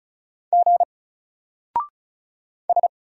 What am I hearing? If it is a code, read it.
Morse code, heard as a keyed tone, about 36 wpm, 700 Hz. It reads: G S
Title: Single Character (Letter:Number:Punctuation:Prosign) Alt 35wpm